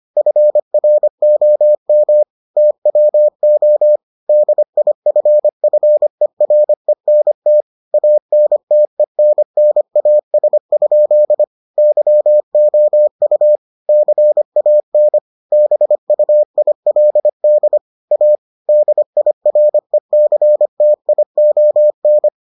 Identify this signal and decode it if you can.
Morse code, heard as a keyed tone, about 25 wpm, 600 Hz. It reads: FROM TWO DIFFERENT ANTENNAS? YOU CAN BUILD A DIRECTION